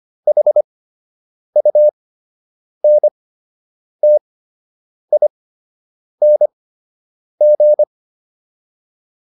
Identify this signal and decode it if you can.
Morse code, heard as a keyed tone, about 25 wpm, spaced as for 8 wpm, 600 Hz. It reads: HUNTING